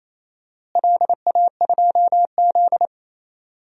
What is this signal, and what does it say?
Morse code, heard as a keyed tone, about 28 wpm, 700 Hz. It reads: LA2Z
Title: DXCC-100 Call Signs - 4 Characters - Repeated 3x - 28wpm